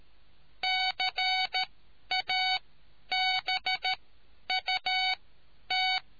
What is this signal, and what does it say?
Morse code, heard as a keyed tone, about 13 wpm, spaced as for 10 wpm, 750 Hz. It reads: CABUT